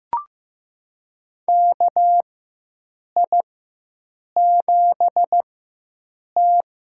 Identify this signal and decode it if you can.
Morse code, heard as a keyed tone, about 15 wpm, spaced as for 7 wpm, 700 Hz. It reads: KI7T
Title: USA Call Signs - 6 Characters - 4x Character Spacing - 15wpm